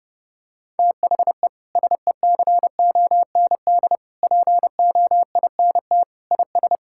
Morse code, heard as a keyed tone, about 30 wpm, 700 Hz. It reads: THE SECOND POINT IS
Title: Sentences from Top 1000 Words - Review Part 5 - 30wpm